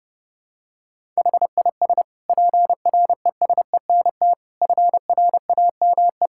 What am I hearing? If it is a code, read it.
Morse code, heard as a keyed tone, about 32 wpm, 700 Hz. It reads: HIS PRESENT FRAME
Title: Sentences from Top 1000 Words - Review Part 6 - 30wpm